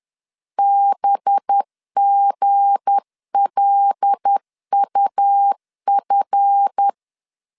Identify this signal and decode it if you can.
Morse code, heard as a keyed tone, about 10 wpm, 800 Hz. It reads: BGLUF
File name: LEZIONI / 10 SIGNS / 10B INT 10wpm 2844